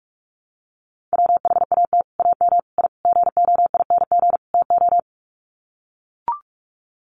Morse code, heard as a keyed tone, about 44 wpm, 700 Hz. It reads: WHAT AM I GOING TO
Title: Sets of 5 Words Encore 45wpm